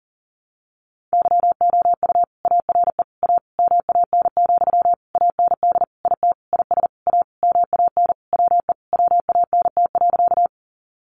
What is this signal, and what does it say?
Morse code, heard as a keyed tone, about 40 wpm, 700 Hz. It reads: YOU ARE A MAN, AND IT IS A MAN WE WANT.
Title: Sentences from Top 200 Words 40wpm